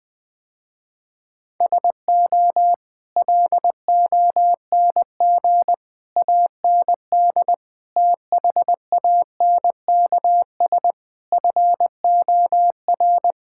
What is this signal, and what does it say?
Morse code, heard as a keyed tone, about 20 wpm, 700 Hz. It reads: SO LONG AND THANKS FOR
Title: Sentences with 8-Word Phrases - Spoken Beginning - 20wpm